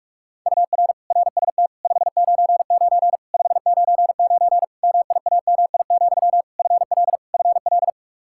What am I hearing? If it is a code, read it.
Morse code, heard as a keyed tone, about 46 wpm, 700 Hz. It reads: UR RST 599 599 MIAMI, FL FL
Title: Practice QSO - Exchange 1 - Beginning - 45wpm